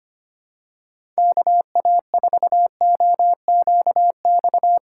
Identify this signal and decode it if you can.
Morse code, heard as a keyed tone, about 25 wpm, 700 Hz. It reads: KA4OQX
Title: USA Call Signs - 6 Characters - 3x Character Spacing - 25wpm